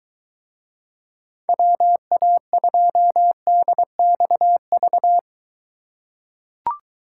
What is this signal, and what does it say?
Morse code, heard as a keyed tone, about 23 wpm, 700 Hz. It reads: WA2DXV